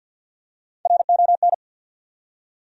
Morse code, heard as a keyed tone, about 52 wpm, 700 Hz. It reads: RON
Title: USA Common Names - Max 3 Letters - 50wpm